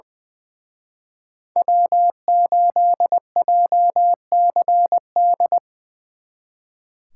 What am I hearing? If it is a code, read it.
Morse code, heard as a keyed tone, about 24 wpm, 700 Hz. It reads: W8JCD
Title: Speed Racing - USA 1x3 Call Signs 20wpm